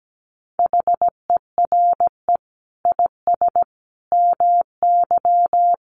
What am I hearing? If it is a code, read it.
Morse code, heard as a keyed tone, about 17 wpm, 700 Hz. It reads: HERE IS MY
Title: Sentences from Top 200 Words 17wpm